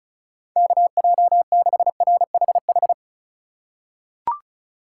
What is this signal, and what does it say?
Morse code, heard as a keyed tone, about 36 wpm, 700 Hz. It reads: KJ6RHH
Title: USA Call Signs - 6 Characters - 4x Character Spacing - 35wpm